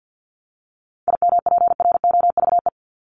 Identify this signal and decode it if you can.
Morse code, heard as a keyed tone, about 50 wpm, 700 Hz. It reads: IMPROVE
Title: Top 501-750 Words 50wpm